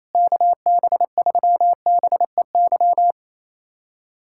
Morse code, heard as a keyed tone, about 28 wpm, 700 Hz. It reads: KB3BEY
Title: USA Callsign DE Callsign - 6 Characters - 28wpm